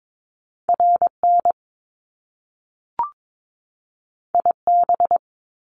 Morse code, heard as a keyed tone, about 22 wpm, 700 Hz. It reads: RN IB
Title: Top Two Letter Combinations in Words 22wpm